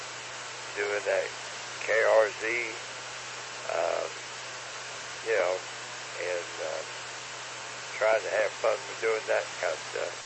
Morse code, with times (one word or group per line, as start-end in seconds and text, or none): none